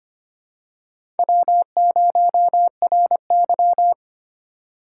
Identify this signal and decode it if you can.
Morse code, heard as a keyed tone, about 25 wpm, 700 Hz. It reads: W0RY
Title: USA Call Signs - 4 Characters - 4x Character Spacing - 25wpm